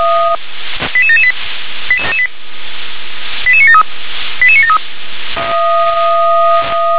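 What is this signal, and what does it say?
Five-tone selective call (ZVEI), heard as quick runs of five tones, tones of about 70 ms. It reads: 88778 88973 88973
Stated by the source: ZVEI five-tone selective call received on an SDR.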